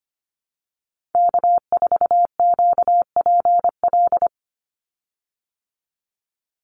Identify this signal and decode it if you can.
Morse code, heard as a keyed tone, about 25 wpm, 700 Hz. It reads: K4QPL
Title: Call Signs Repeated 2x - 25wpm